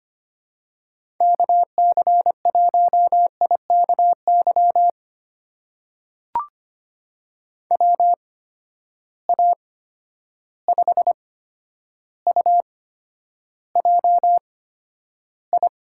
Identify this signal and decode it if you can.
Morse code, heard as a keyed tone, about 25 wpm, spaced as for 11 wpm, 700 Hz. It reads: KC1IKY WA5UJI
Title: USA Call Signs - 6 Characters - 8x Character Spacing - 25wpm